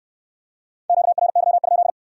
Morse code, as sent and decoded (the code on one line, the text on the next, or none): --.- .-. -.-- ..--..
QRY?